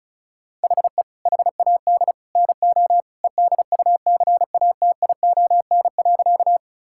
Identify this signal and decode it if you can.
Morse code, heard as a keyed tone, about 36 wpm, 700 Hz. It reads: HE HAD NO EDUCATION.
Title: Sentences from Top 1000 Words - Review Part 6 - 35wpm